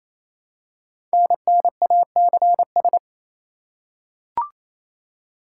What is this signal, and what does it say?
Morse code, heard as a keyed tone, about 27 wpm, 700 Hz. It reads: NNACS
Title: Code Groups - 5 Letters - 28wpm